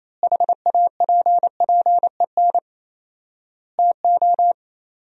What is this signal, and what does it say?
Morse code, heard as a keyed tone, about 28 wpm, 700 Hz. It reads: HAPPEN TO
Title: 4x Word-Spacing - Sentences from Top 500 Words - 28wpm